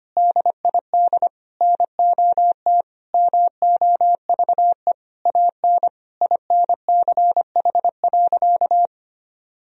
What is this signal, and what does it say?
Morse code, heard as a keyed tone, about 25 wpm, 700 Hz. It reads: DID NOT MOVE AN INCH.
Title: Sentences from Top 1000 Words - Review Part 3 - 25wpm